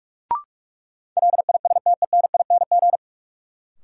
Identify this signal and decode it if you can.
Morse code, heard as a keyed tone, about 44 wpm, 700 Hz. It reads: LISTENING
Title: Sets of 5 Words - New Word Drill - 45wpm